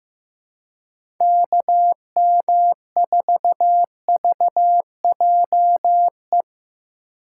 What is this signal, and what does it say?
Morse code, heard as a keyed tone, about 15 wpm, 700 Hz. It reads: KM4VJE
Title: USA Call Signs - 6 Characters - 2x Character Spacing - 15wpm